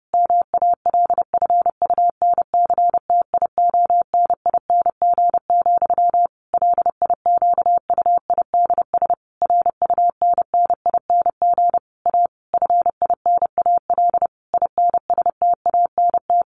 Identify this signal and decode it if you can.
Morse code, heard as a keyed tone, about 30 wpm, 700 Hz. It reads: MALFUNCTIONING, LIQUIDS RUNNING A FINAL INSTANT
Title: A Sound Of Thunder 30wpm